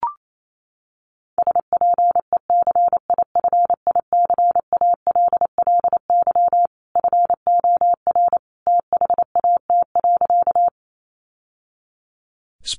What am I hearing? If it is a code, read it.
Morse code, heard as a keyed tone, about 28 wpm, 700 Hz. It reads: SPECIFICALLY FOR THAT.
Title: Sentences from Top 1100 Words 28wpm